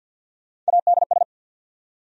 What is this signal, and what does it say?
Morse code, heard as a keyed tone, about 69 wpm, spaced as for 59 wpm, 700 Hz. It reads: ADS
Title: Speed-Racing Abbreviations - 50wpm